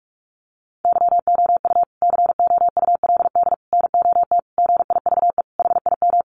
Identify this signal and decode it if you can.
Morse code, heard as a keyed tone, about 44 wpm, 700 Hz. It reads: YOU COULD NOT GIVE HIM
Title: Sentences from Top 1500 Words 45wpm